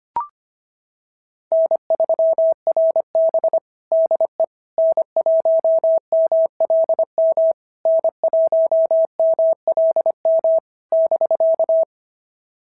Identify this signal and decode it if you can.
Morse code, heard as a keyed tone, about 23 wpm, 650 Hz. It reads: N3RB DE N1MLM N1MLM <BK>
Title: SOTA QSO Practice 25wpm